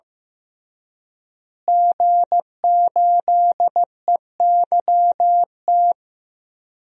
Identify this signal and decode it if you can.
Morse code, heard as a keyed tone, about 15 wpm, 700 Hz. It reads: G8EYT
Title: DXCC-100 Call Signs - 5 Characters - Repeated 3x - 15wpm